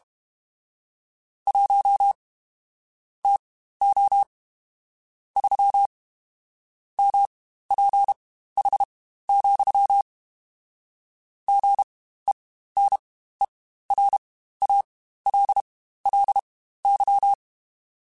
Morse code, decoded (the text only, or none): 1 TO 3 MPH, GENERALLY